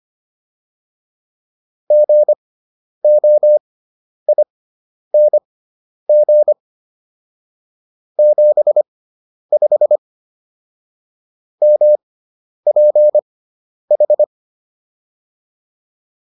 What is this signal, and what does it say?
Morse code, heard as a keyed tone, about 25 wpm, spaced as for 10 wpm, 600 Hz. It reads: GOING 75 MPH